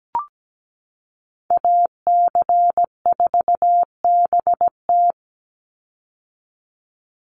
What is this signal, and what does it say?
Morse code, heard as a keyed tone, about 17 wpm, 700 Hz. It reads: AC4BT
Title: Call Signs - 17wpm